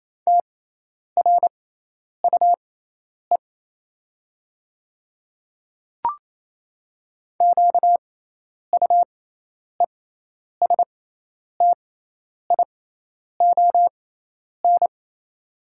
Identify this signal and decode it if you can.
Morse code, heard as a keyed tone, about 28 wpm, spaced as for 10 wpm, 700 Hz. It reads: TRUE QUESTION
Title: 6x Character Spacing - Top 500 Words - 28wpm